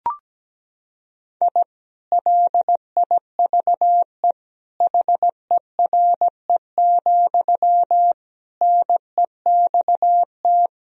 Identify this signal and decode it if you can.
Morse code, heard as a keyed tone, about 17 wpm, 700 Hz. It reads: I LIVE HERE, NEXT
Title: Sentences from Top 1000 Words - Review Part 1 - 17wpm